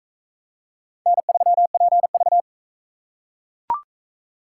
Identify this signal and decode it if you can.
Morse code, heard as a keyed tone, about 39 wpm, 700 Hz. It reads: N3PV